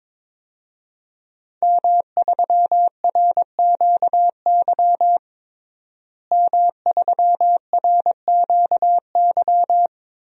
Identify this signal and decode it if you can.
Morse code, heard as a keyed tone, about 22 wpm, 700 Hz. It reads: M3RQY M3RQY
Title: DXCC-100 Call Signs - 5 Characters - Repeated 2x - 22wpm